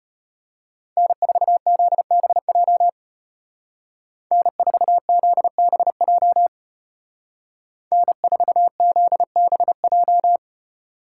Morse code, heard as a keyed tone, about 30 wpm, 700 Hz. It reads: N4ZBJ N4ZBJ N4ZBJ
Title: Speed Racing - USA 1x3 Call Signs 25wpm